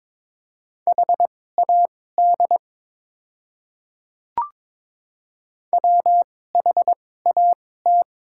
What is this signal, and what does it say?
Morse code, heard as a keyed tone, about 21 wpm, spaced as for 16 wpm, 700 Hz. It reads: HAD WHAT
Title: 2x Character Spacing - Top 100 Words - 22wpm